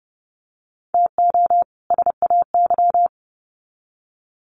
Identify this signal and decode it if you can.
Morse code, heard as a keyed tone, about 30 wpm, 700 Hz. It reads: TO SAY
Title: Sets of 2 Words 30wpm